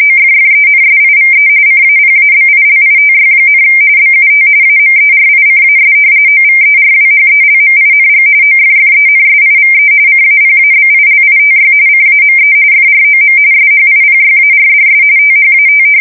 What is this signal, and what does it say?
RTTY, about 45.45 baud. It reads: OFFENSIVE CAPABILITY, BUT AMERICAS OVERWHELMING OFFENSIVE CAPABILITY CAME FROM SHIPYARDS THAT
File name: fsk170 / usnavy4